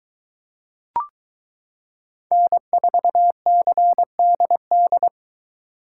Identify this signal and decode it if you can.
Morse code, heard as a keyed tone, about 23 wpm, 700 Hz. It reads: N4CDD